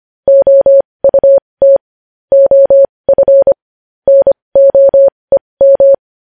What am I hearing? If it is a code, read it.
Morse code, heard as a keyed tone, about 25 wpm, spaced as for 20 wpm, 550 Hz. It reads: OUT OF NOEM
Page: news-headlines-in-morse-code-at-20-wpm